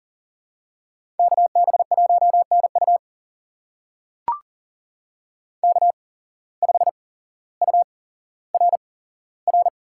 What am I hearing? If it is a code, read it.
Morse code, heard as a keyed tone, about 43 wpm, spaced as for 17 wpm, 700 Hz. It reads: KB1NU K5URR
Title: USA Call Signs - 5 Characters - 8x Character Spacing - 40wpm